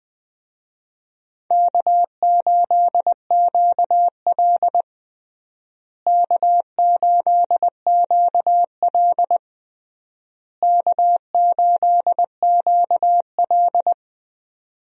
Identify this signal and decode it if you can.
Morse code, heard as a keyed tone, about 20 wpm, 700 Hz. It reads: K8QL K8QL K8QL
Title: USA Call Signs - 4 Characters - Repeated 3x - 20wpm